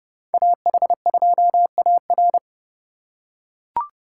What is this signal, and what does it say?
Morse code, heard as a keyed tone, about 28 wpm, 700 Hz. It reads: AH2AR